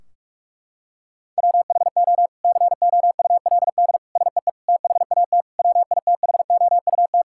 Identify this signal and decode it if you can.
Morse code, heard as a keyed tone, about 46 wpm, 700 Hz. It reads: WHO COULD SEE THAT WITHOUT
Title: Sentences from Top 1600 Words 45wpm